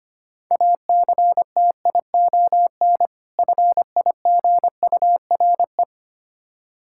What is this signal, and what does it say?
Morse code, heard as a keyed tone, about 25 wpm, 700 Hz. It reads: ACTION FIGURE
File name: practice / 25wpm / Difficult Words with Hints - 25wpm